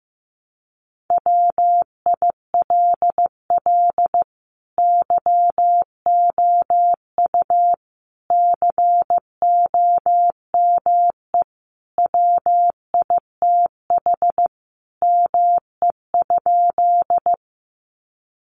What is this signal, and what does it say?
Morse code, heard as a keyed tone, about 15 wpm, 700 Hz. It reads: WILL YOU COME WITH ME?
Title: Sentences from Top 200 Words 15wpm